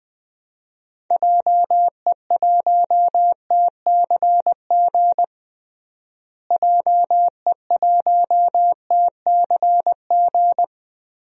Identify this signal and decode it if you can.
Morse code, heard as a keyed tone, about 20 wpm, 700 Hz. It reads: JE1TCG JE1TCG
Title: DXCC-100 Call Signs - 6 Characters - Repeated 3x - 20wpm